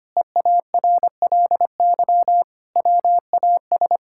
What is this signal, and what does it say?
Morse code, heard as a keyed tone, about 25 wpm, 700 Hz. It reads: EARLY WAS
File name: practice / 25wpm / Sentences with 4-Word Phrases - 25wpm